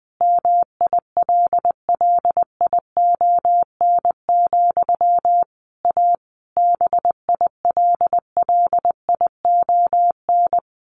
Morse code, heard as a keyed tone, about 20 wpm, 700 Hz. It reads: MILLION, A BILLION
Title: A Sound Of Thunder 20wpm